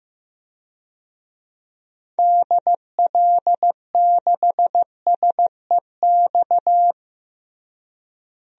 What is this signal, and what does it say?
Morse code, heard as a keyed tone, about 15 wpm, 700 Hz. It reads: DL6SEX
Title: DXCC-100 Call Signs - 6 Characters - Repeated 2x - 15wpm